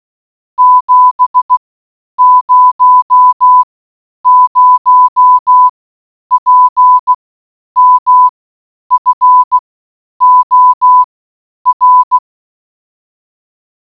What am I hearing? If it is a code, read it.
Morse code, heard as a keyed tone, about 16 wpm, spaced as for 9 wpm, 1000 Hz. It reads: 700 PM FOR